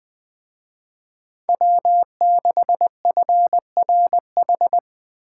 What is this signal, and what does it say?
Morse code, heard as a keyed tone, about 20 wpm, 700 Hz. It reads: W6FRH